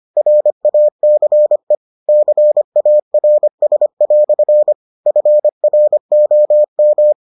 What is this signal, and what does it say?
Morse code, heard as a keyed tone, about 25 wpm, 600 Hz. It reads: RACE CARS" FROM